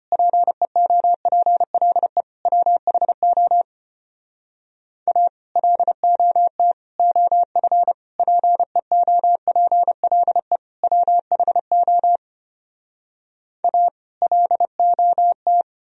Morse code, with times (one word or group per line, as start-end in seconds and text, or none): 0.1-2.2 PEOPLE
2.5-3.6 WHO
5.1-5.3 A
5.6-6.7 LOT
7.0-7.9 OF
8.2-10.6 PEOPLE
10.8-12.2 WHO
13.6-13.9 A
14.2-15.6 LOT